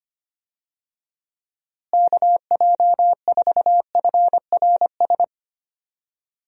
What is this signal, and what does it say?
Morse code, heard as a keyed tone, about 25 wpm, 700 Hz. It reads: KJ4FRS
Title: USA Call Signs - 6 Characters - Repeated 3x - 25wpm